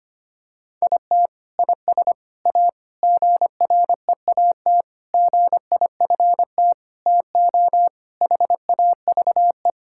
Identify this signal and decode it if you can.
Morse code, heard as a keyed tone, about 25 wpm, 700 Hz. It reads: IT IS A GREAT GIFT TO HAVE
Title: Sentences from Top 1000 Words - Review Part 6 - 25wpm